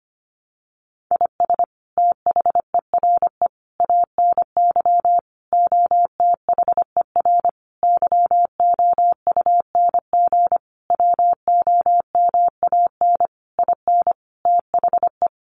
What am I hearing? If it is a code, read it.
Morse code, heard as a keyed tone, about 25 wpm, 700 Hz. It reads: IS THERE ANY OTHER YOUNG WOMAN IN THE
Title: Sentences from Top 1100 Words 25wpm